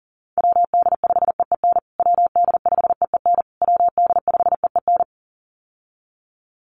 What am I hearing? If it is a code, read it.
Morse code, heard as a keyed tone, about 37 wpm, 700 Hz. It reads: WD5EEN WD5EEN WD5EEN
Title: USA Call Signs Repeated 3x 40wpm